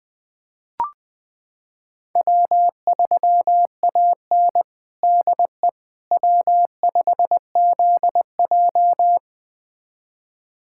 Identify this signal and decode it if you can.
Morse code, heard as a keyed tone, about 21 wpm, 700 Hz. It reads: W3AN DE W5ZJ
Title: USA Callsign DE Callsign - 4 Characters - 20wpm